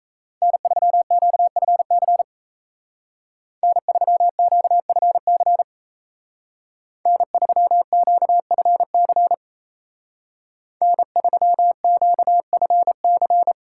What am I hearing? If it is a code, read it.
Morse code, heard as a keyed tone, about 33 wpm, 700 Hz. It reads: N3QFC N3QFC N3QFC N3QFC
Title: Speed Racing - USA 1x3 Call Signs 28wpm